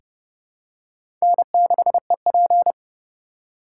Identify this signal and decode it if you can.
Morse code, heard as a keyed tone, about 30 wpm, 700 Hz. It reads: N6EP
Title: USA Call Signs - 4 Characters - 4x Character Spacing - 30wpm